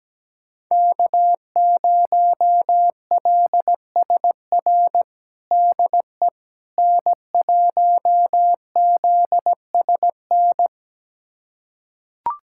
Morse code, heard as a keyed tone, about 17 wpm, 700 Hz. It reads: K0LSR DE N1ZSN
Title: USA Callsign DE Callsign - 5 Characters - 17wpm